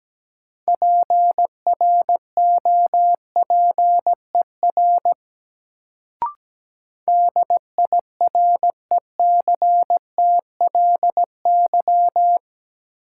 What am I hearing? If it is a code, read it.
Morse code, heard as a keyed tone, about 17 wpm, 700 Hz. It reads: PROPER DIRECTLY
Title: Sentences from Top 800 Words - New Word Drill - 17wpm